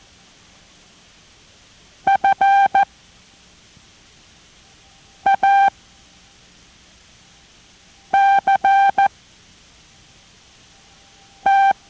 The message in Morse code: FACT